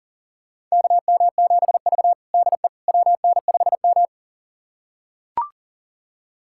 Morse code, heard as a keyed tone, about 40 wpm, 700 Hz. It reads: KM7V DE WN5M
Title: USA Callsign DE Callsign - 4 Characters - 40wpm